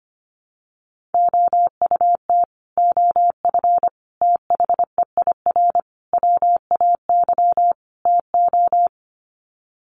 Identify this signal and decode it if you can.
Morse code, heard as a keyed tone, about 25 wpm, 700 Hz. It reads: OUT OF THEIR WAY TO
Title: Sets of 5 Words Encore2 25wpm